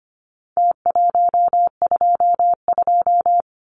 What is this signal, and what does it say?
Morse code, heard as a keyed tone, about 25 wpm, 700 Hz. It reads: T122
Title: Numbers - Contest Style - 4-Digit 25wpm